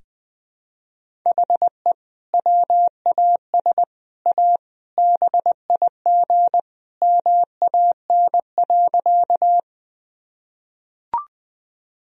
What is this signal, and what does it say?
Morse code, heard as a keyed tone, about 21 wpm, 700 Hz. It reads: HE WAS A BIG MAN.
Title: Sentences from Top 1000 Words - Review Part 1 - 20wpm